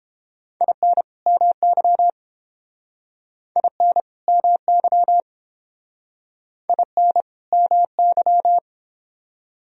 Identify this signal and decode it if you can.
Morse code, heard as a keyed tone, about 30 wpm, 700 Hz. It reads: IN MY IN MY IN MY